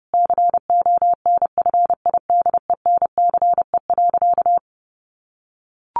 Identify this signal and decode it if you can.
Morse code, heard as a keyed tone, about 30 wpm, 700 Hz. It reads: CONFIDENCE.
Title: Sentences from Top 1500 Words 30wpm